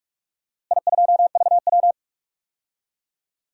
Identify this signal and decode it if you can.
Morse code, heard as a keyed tone, about 45 wpm, 700 Hz. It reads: I2VW
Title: DXCC-100 Call Signs - 4 Characters - Repeated 2x - 45wpm